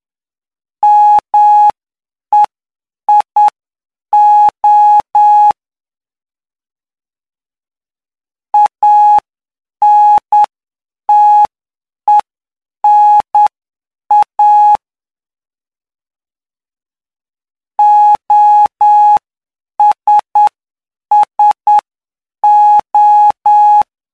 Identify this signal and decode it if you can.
Morse code, heard as a keyed tone, about 9 wpm, spaced as for 7 wpm, 840 Hz. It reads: MEIO ANTENA OSSO
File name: cw / py2lcd / 06-PY2LCD-Li��o 15